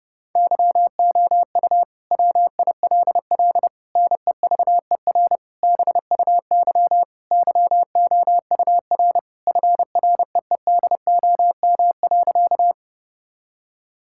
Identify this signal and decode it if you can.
Morse code, heard as a keyed tone, about 30 wpm, 700 Hz. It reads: YOU WILL NEVER BUY YOUR FREEDOM.
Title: Sentences from Top 1000 Words - Review Part 5 - 30wpm